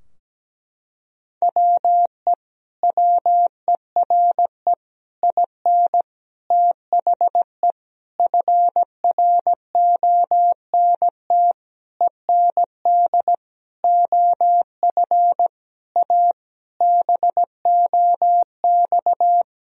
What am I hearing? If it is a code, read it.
Morse code, heard as a keyed tone, about 17 wpm, 700 Hz. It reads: WE WERE IN THE FRONT END OF A BOX
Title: Sentences from Top 1000 Words - Review Part 2 - 17wpm